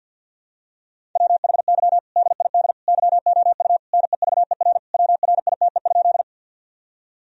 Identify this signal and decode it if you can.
Morse code, heard as a keyed tone, about 52 wpm, 700 Hz. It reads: WHY DID YOU NEVER WRITE?
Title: Sentences from Top 1000 Words - Review Part 1 - 50wpm